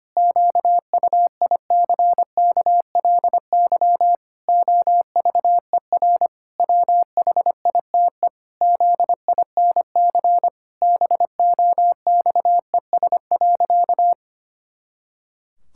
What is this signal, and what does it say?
Morse code, heard as a keyed tone, about 25 wpm, 700 Hz. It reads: QUICKLY OVER WHITE ZINC BOXES.
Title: Pangram Challenge v2 - 25wpm